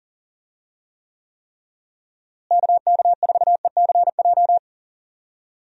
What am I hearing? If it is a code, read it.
Morse code, heard as a keyed tone, about 40 wpm, 700 Hz. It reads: KK4ECJ
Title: USA Call Signs - 6 Characters - Repeated 3x - 40wpm